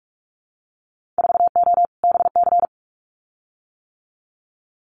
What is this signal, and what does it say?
Morse code, heard as a keyed tone, about 46 wpm, 690 Hz. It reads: 4O BC
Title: Winter Field Day Exchanges Encore 45wpm